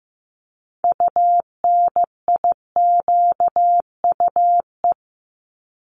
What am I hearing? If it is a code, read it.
Morse code, heard as a keyed tone, about 15 wpm, 700 Hz. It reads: UNIQUE
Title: Top 751-1000 Words 15wpm